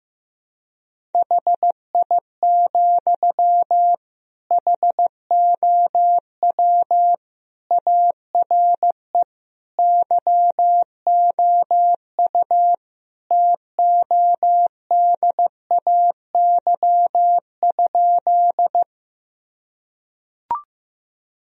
Morse code, heard as a keyed tone, about 15 wpm, 700 Hz. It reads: HI, HOW ARE YOU TODAY?
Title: Sentences from Top 1000 Words - Review Part 3 - 15wpm